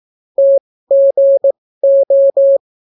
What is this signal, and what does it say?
Morse code, heard as a keyed tone, about 18 wpm, spaced as for 14 wpm, 550 Hz. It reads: TGO